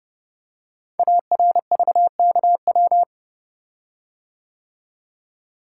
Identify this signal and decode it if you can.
Morse code, heard as a keyed tone, about 30 wpm, 690 Hz. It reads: ARVKW
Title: Code Groups - 5 Letters - 30wpm